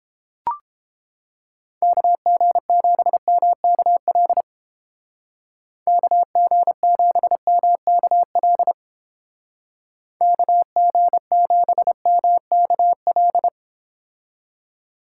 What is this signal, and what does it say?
Morse code, heard as a keyed tone, about 30 wpm, 700 Hz. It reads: KG7MKL KG7MKL KG7MKL